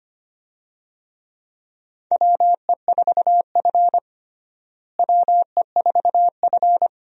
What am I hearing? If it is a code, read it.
Morse code, heard as a keyed tone, about 25 wpm, 700 Hz. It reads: WE4F WE4F
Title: USA Call Signs - 4 Characters - Repeated 2x - 25wpm